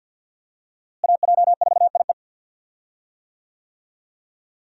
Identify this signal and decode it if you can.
Morse code, heard as a keyed tone, about 47 wpm, 700 Hz. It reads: AJ4IE
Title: USA Call Signs - 5 Characters - Repeated 3x - 50wpm